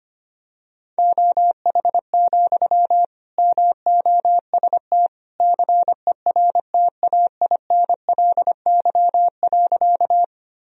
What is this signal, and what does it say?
Morse code, heard as a keyed tone, about 25 wpm, 700 Hz. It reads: OH, MOST CERTAINLY.
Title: Sentences from Top 1000 Words - Review Part 1 - 25wpm